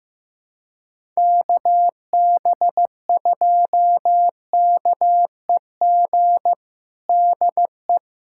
Morse code, heard as a keyed tone, about 15 wpm, 700 Hz. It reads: KB2KEG DE
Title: USA Callsign DE Callsign 15wpm